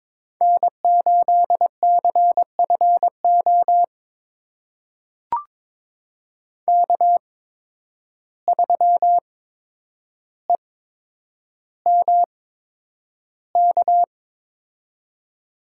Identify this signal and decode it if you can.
Morse code, heard as a keyed tone, about 22 wpm, spaced as for 9 wpm, 700 Hz. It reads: N8CFO K3EMK
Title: USA Call Signs - 5 Characters - 8x Character Spacing - 22wpm